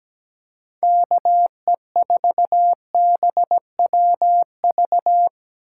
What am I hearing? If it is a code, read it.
Morse code, heard as a keyed tone, about 17 wpm, 700 Hz. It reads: KE4BWV
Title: USA Call Signs - 6 Characters - 2x Character Spacing - 17wpm